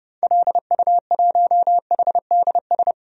Morse code, heard as a keyed tone, about 32 wpm, 700 Hz. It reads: LU1HDS
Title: DXCC-100 Call Signs - 6 Characters - Repeated 3x - 30wpm